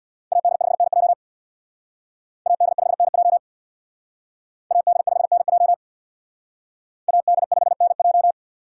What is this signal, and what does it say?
Morse code, heard as a keyed tone, about 60 wpm, 700 Hz. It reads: AD5NJ AD5NJ AD5NJ AD5NJ
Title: Speed Racing - USA 2x2 Call Signs 50wpm